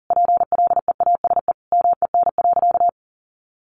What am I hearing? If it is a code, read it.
Morse code, heard as a keyed tone, about 37 wpm, 700 Hz. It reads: PLEASE MEN.
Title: Sentences from Top 1500 Words 40wpm